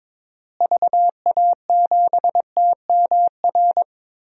Morse code, heard as a keyed tone, about 22 wpm, 700 Hz. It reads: VA7TMR
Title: DXCC-100 Call Signs - 6 Characters - Repeated 3x - 22wpm